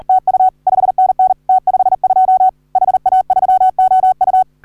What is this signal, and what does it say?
Morse code, heard as a keyed tone, about 40 wpm, 750 Hz. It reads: TU 5NN T52 HA3OU
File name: CW / HA3OU RHOBCW REC